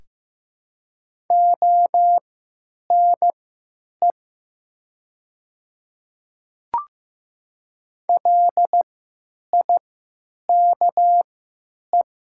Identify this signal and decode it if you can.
Morse code, heard as a keyed tone, about 15 wpm, spaced as for 9 wpm, 700 Hz. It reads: ONE LIKE
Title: 3x Character Spacing - Top 200 Words - 15wpm